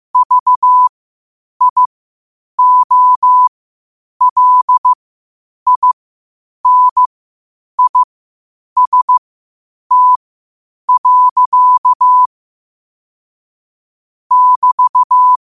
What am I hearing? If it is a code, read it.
Morse code, heard as a keyed tone, about 15 wpm, spaced as for 8 wpm, 1000 Hz. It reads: VIOLINIST. <BT>